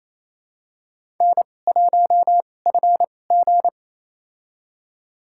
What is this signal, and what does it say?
Morse code, heard as a keyed tone, about 28 wpm, spaced as for 20 wpm, 700 Hz. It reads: N1FG
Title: USA Call Signs - 4 Characters - 2x Character Spacing - 28wpm